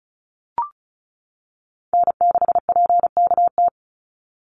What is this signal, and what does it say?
Morse code, heard as a keyed tone, about 34 wpm, 700 Hz. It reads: N6PKT
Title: NA SOTA Chaser Callsigns 35wpm